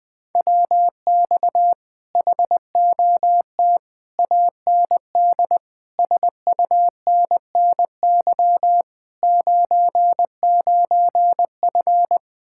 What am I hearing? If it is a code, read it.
Morse code, heard as a keyed tone, about 20 wpm, 700 Hz. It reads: WX HOT AND SUNNY 99F
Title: Speed-Racing Common QSO Phrases - 17wpm